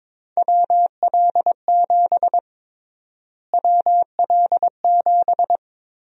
Morse code, heard as a keyed tone, about 22 wpm, 700 Hz. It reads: WL7 WL7
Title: USA Call Sign Prefixes - Repeated 2x - 22wpm